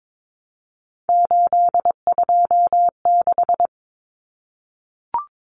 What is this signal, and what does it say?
Morse code, heard as a keyed tone, about 22 wpm, 700 Hz. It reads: 826
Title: Common Cut Numbers - 3-Digit 22wpm